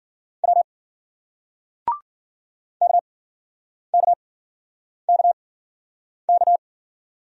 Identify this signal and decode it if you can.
Morse code, heard as a keyed tone, about 50 wpm, spaced as for 9 wpm, 700 Hz. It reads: W KKKK
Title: Speed-Racing Single Letters 40wpm